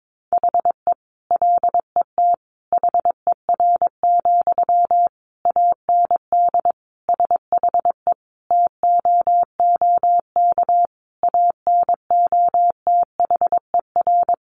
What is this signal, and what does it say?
Morse code, heard as a keyed tone, about 22 wpm, 700 Hz. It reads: HE LET HER, AND SHE TOOK ANOTHER
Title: Sentences from Top 1500 Words 22wpm